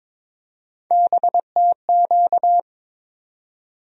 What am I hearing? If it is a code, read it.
Morse code, heard as a keyed tone, about 21 wpm, 700 Hz. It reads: BTQ
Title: USA Call Sign Postfixes - 3 Letters 22wpm